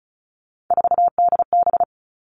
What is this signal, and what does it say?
Morse code, heard as a keyed tone, about 34 wpm, 700 Hz. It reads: 4DB
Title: All Cut Numbers - 3-Digit 35wpm